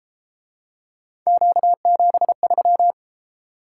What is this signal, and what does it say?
Morse code, heard as a keyed tone, about 30 wpm, 700 Hz. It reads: Q73